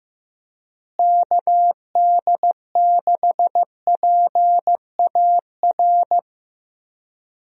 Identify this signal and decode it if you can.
Morse code, heard as a keyed tone, about 15 wpm, 700 Hz. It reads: KD6PAR
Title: USA Call Signs - 6 Characters - Repeated 3x - 15wpm